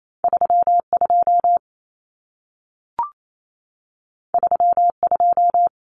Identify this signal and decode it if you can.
Morse code, heard as a keyed tone, about 28 wpm, 700 Hz. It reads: 32 32
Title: Numbers - 2-Digit 28wpm